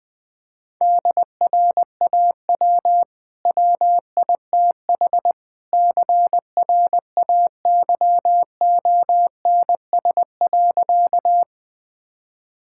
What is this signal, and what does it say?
Morse code, heard as a keyed tone, about 20 wpm, 700 Hz. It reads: DRAW WITH CRAYONS.
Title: Easy Sentences with Hints - 20wpm